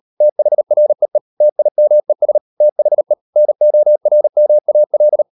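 Morse code, heard as a keyed tone, about 38 wpm, 600 Hz. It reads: THREE TIMES THE NORMAL
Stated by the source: Morse code practice at 38 WPM, sending Ken Follett's novel Never in English, from CW med Gnister.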